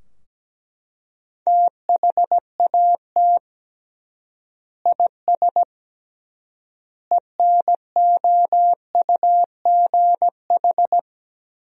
Morse code, heard as a keyed tone, about 17 wpm, 700 Hz. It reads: THAT IS ENOUGH
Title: 3x Word-Spacing - Sentences from Top 400 Words - 17wpm